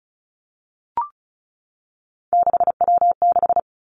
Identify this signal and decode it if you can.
Morse code, heard as a keyed tone, about 36 wpm, 700 Hz. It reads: 6W6
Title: Three Challenging Letter-Number Combinations 35wpm